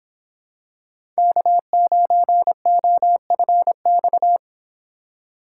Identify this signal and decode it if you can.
Morse code, heard as a keyed tone, about 25 wpm, 700 Hz. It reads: K9OFX